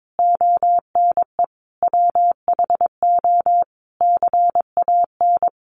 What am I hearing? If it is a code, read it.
Morse code, heard as a keyed tone, about 22 wpm, 700 Hz. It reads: ONE WHO CAN
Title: Sentences from Top 200 Words 22wpm